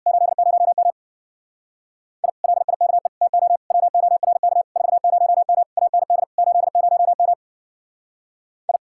Morse code, heard as a keyed tone, about 59 wpm, 700 Hz. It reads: 80M I LIKE TO WORK 40M AND 80M I
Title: Speed-Racing Common QSO Phrases - 50wpm